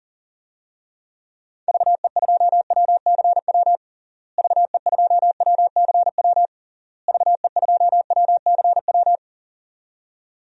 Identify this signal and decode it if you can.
Morse code, heard as a keyed tone, about 40 wpm, 700 Hz. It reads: VE2WCW VE2WCW VE2WCW
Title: DXCC-100 Call Signs - 6 Characters - Repeated 3x - 40wpm